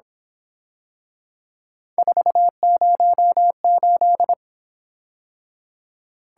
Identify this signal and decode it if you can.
Morse code, heard as a keyed tone, about 26 wpm, 700 Hz. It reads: 408
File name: Speed-Racing - Three Letter-Number Combinations - 22wpm